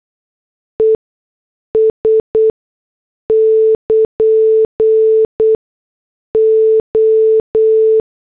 Morse code, heard as a keyed tone, about 8 wpm, spaced as for 6 wpm, 440 Hz. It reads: ES <KN> O
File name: morse / headlines